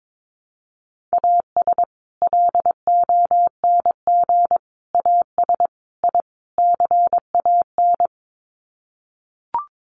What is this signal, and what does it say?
Morse code, heard as a keyed tone, about 22 wpm, 700 Hz. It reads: AS LONG AS I CAN
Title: Sets of 5 Words Encore2 22wpm